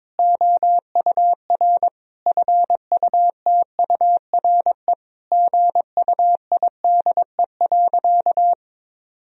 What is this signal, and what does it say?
Morse code, heard as a keyed tone, about 22 wpm, 700 Hz. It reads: OUR FUTURE GUIDE.
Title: Sentences from Top 1000 Words - Review Part 3 - 22wpm